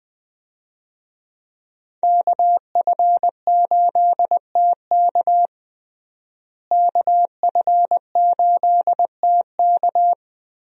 Morse code, heard as a keyed tone, about 20 wpm, 700 Hz. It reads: KF8TK KF8TK
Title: USA Call Signs - 5 Characters - Repeated 3x - 20wpm